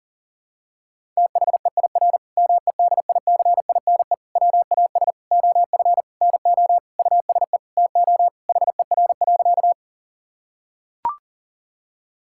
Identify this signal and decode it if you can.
Morse code, heard as a keyed tone, about 40 wpm, 700 Hz. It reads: THEIR MEDICINE WAS OF NO USE TO HER.
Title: Sentences from Top 1000 Words - Review Part 6 - 40wpm